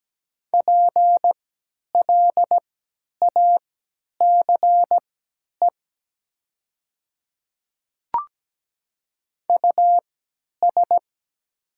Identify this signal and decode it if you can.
Morse code, heard as a keyed tone, about 17 wpm, spaced as for 10 wpm, 700 Hz. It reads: PLACE US
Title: 3x Character Spacing - Top 200 Words - 17wpm